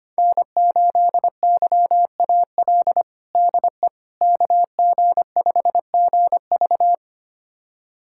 Morse code, heard as a keyed tone, about 25 wpm, 700 Hz. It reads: N8YAL DE KG5GV
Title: USA Callsign DE Callsign - 5 Characters - 25wpm